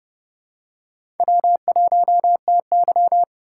Morse code, heard as a keyed tone, about 30 wpm, 700 Hz. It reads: W1TY
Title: USA Call Signs - 4 Characters - 2x Character Spacing - 30wpm